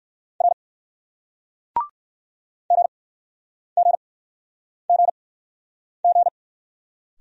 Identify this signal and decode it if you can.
Morse code, heard as a keyed tone, about 61 wpm, spaced as for 9 wpm, 700 Hz. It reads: R GGGG